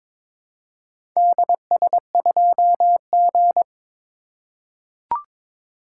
Morse code, heard as a keyed tone, about 22 wpm, 700 Hz. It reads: DS2G
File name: practice / 22wpm / Four Letter-Number Combinations - 22wpm